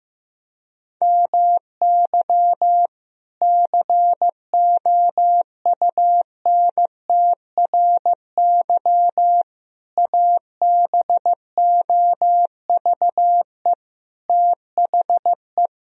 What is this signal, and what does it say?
Morse code, heard as a keyed tone, about 15 wpm, 700 Hz. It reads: MY COUNTRY ABOVE THE
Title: Sentences from Top 1000 Words - Review Part 2 - 15wpm